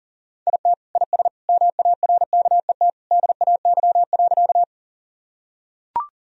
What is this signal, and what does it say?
Morse code, heard as a keyed tone, about 40 wpm, 700 Hz. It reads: IT IS MARKET DAY.
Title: Sentences from Top 1000 Words - Review Part 3 - 40wpm